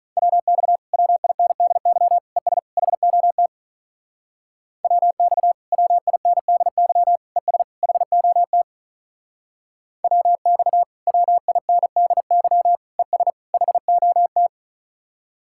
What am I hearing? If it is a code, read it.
Morse code, heard as a keyed tone, about 41 wpm, 700 Hz. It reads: WX WINDY ES HOT WX WINDY ES HOT WX WINDY ES HOT
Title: Speed-Racing Common QSO Phrases - 35wpm